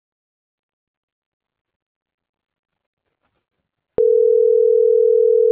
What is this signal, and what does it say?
Ringback tone (North America), 440 + 480 Hz.